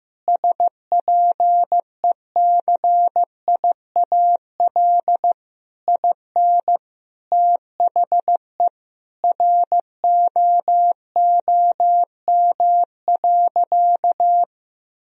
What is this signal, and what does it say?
Morse code, heard as a keyed tone, about 15 wpm, 700 Hz. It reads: SPECIAL IN THE ROOM.
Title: Sentences from Top 1000 Words - Review Part 3 - 15wpm